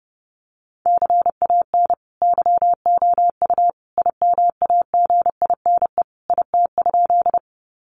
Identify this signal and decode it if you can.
Morse code, heard as a keyed tone, about 30 wpm, 700 Hz. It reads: CAN YOU IMAGINE IT?
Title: Sentences from Top 1400 Words 30wpm